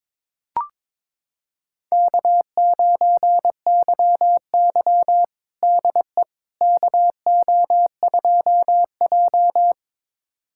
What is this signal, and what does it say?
Morse code, heard as a keyed tone, about 22 wpm, 700 Hz. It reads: K9YY DE KO2J
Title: USA Callsign DE Callsign - 4 Characters - 22wpm